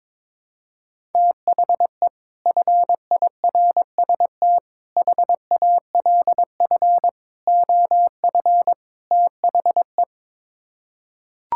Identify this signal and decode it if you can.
Morse code, heard as a keyed tone, about 22 wpm, 700 Hz. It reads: THE FIRST HALF OF THE
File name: practice / 22wpm / Build A Phrase - 5-Word Phrases - 22wpm